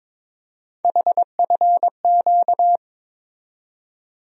Morse code, heard as a keyed tone, about 22 wpm, 700 Hz. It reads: HFQ
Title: USA Call Sign Postfixes - 3 Letters 22wpm